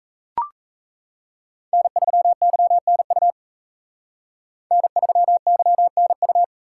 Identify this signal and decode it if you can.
Morse code, heard as a keyed tone, about 36 wpm, 700 Hz. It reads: N3YNU N3YNU